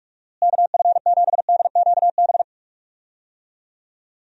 Morse code, heard as a keyed tone, about 46 wpm, 700 Hz. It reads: KF7DQB
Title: USA Call Signs - 6 Characters - Repeated 2x - 45wpm